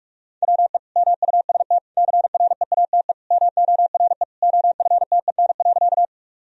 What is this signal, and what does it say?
Morse code, heard as a keyed tone, about 46 wpm, 700 Hz. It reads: WE MUST CREATE MORE OFTEN.
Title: Sentences from Top 1000 Words - Review Part 3 - 45wpm